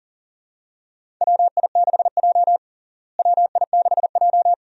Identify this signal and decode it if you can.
Morse code, heard as a keyed tone, about 40 wpm, 700 Hz. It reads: WI6J WI6J
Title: USA Call Signs - 4 Characters - Repeated 2x - 40wpm